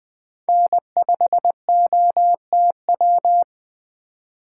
Morse code, heard as a keyed tone, about 20 wpm, 700 Hz. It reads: N5OTW
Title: Speed Racing - USA 1x3 Call Signs 17wpm